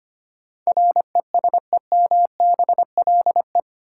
Morse code, heard as a keyed tone, about 25 wpm, 700 Hz. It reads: RESEMBLE
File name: Eight Letter Words - 25wpm